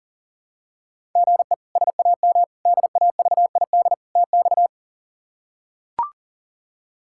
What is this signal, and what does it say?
Morse code, heard as a keyed tone, about 37 wpm, 700 Hz. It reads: GE SAM DAVID TX
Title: K1USN SST QSO Practice - 40wpm